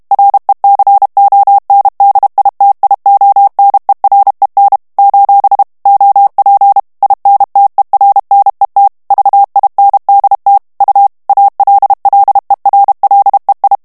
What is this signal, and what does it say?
Morse code, heard as a keyed tone, about 32 wpm, 800 Hz. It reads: RECONDITIONEREN: OP INTERNET VINDT U ALLERLEI
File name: HOLLANDAIS texte clair 2 vitesse 30 mots minute